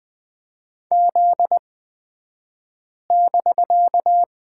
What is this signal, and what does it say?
Morse code, heard as a keyed tone, about 20 wpm, spaced as for 5 wpm, 700 Hz. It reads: Z<BK>
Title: 249 Lesson 40d - BK - Two Characters 20wpm